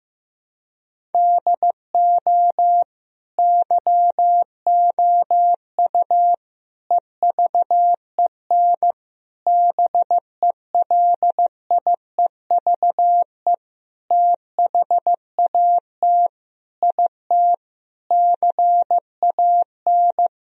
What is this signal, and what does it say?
Morse code, heard as a keyed tone, about 15 wpm, 700 Hz. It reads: DO YOU EVEN BELIEVE THAT IT CAN
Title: Sentences from Top 1000 Words - Review Part 1 - 15wpm